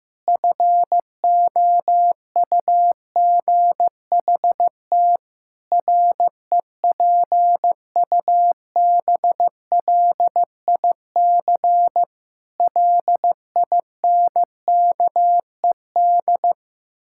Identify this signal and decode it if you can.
Morse code, heard as a keyed tone, about 15 wpm, 700 Hz. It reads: FOUGHT REPUBLIC LINKED
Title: Pangram Challenge v2 - 15wpm